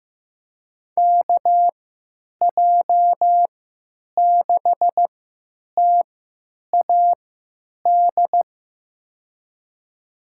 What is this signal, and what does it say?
Morse code, heard as a keyed tone, about 15 wpm, spaced as for 9 wpm, 700 Hz. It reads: KJ6TAD
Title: USA Call Signs - 3x Character Spacing - 15wpm